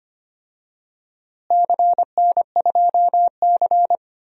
Morse code, heard as a keyed tone, about 25 wpm, 700 Hz. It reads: CN2C
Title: DXCC-100 Call Signs - 4 Characters - Repeated 3x - 25wpm